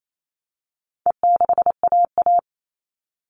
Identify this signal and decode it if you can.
Morse code, heard as a keyed tone, about 28 wpm, 700 Hz. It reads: E6AA
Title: All Cut Numbers - 4-Digit 28wpm